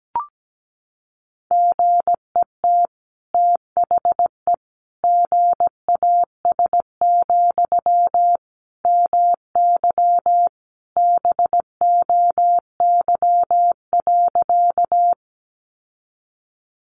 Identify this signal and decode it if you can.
Morse code, heard as a keyed tone, about 17 wpm, 700 Hz. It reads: GET THE GAS, MY BOY.
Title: Sentences from Top 1100 Words 17wpm